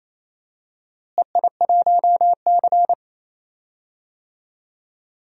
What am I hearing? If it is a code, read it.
Morse code, heard as a keyed tone, about 28 wpm, 700 Hz. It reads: EI1C
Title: DXCC-100 Call Signs - 4 Characters - Repeated 2x - 28wpm